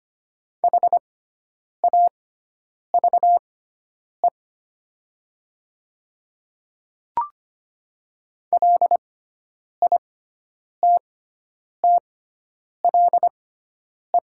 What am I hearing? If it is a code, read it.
Morse code, heard as a keyed tone, about 25 wpm, spaced as for 9 wpm, 700 Hz. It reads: HAVE LITTLE
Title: 6x Character Spacing - Top 200 Words - 25wpm